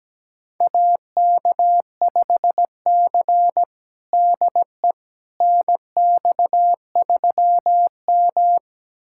Morse code, heard as a keyed tone, about 17 wpm, 700 Hz. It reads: AK5C DE NX3M
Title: USA Callsign DE Callsign - 4 Characters - 17wpm